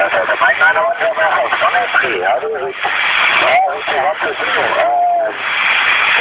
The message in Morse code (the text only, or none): none